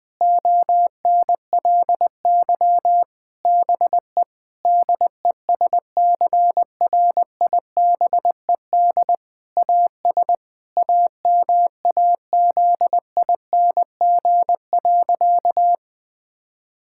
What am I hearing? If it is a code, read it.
Morse code, heard as a keyed tone, about 21 wpm, 700 Hz. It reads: ONLY BE DESCRIBED AS AMAZING.
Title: Sentences with 6-Word Phrases - Spoken Beginning - 20wpm